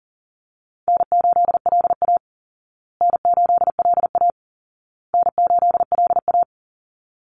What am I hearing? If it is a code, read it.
Morse code, heard as a keyed tone, about 40 wpm, 700 Hz. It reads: N8LA N8LA N8LA
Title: Easy Call Signs Repeated 3x - 40wpm